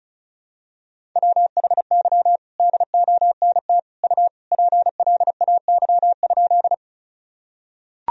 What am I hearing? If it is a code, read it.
Morse code, heard as a keyed tone, about 36 wpm, 700 Hz. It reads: WHY DONT U PLAY?
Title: Sentences from Top 300 Words and Abbreviations 35wpm